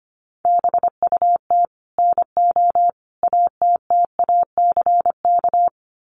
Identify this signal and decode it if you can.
Morse code, heard as a keyed tone, about 25 wpm, 700 Hz. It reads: BUT NO ATTACK
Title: Sentences from Top 1500 Words 25wpm